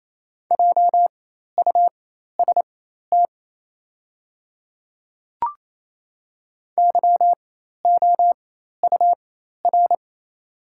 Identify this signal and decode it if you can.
Morse code, heard as a keyed tone, about 28 wpm, spaced as for 13 wpm, 700 Hz. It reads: JUST YOUR
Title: 4x Character Spacing - Top 300 Words - 28wpm